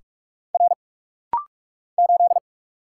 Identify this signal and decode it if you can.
Morse code, heard as a keyed tone, about 46 wpm, 700 Hz. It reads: R 8
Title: Single Letter-Number - Rapid-Fire 45wpm